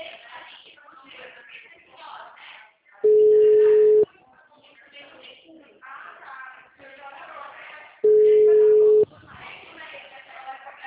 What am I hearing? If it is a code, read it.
Ringback tone (Europe, 425 Hz), with speech.